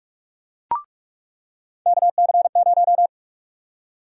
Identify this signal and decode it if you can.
Morse code, heard as a keyed tone, about 46 wpm, 700 Hz. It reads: KC0
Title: USA Call Sign Prefixes - 45wpm